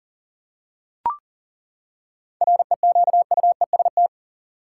Morse code, heard as a keyed tone, about 40 wpm, 700 Hz. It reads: REQUEST
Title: Seven Letter Words - 40wpm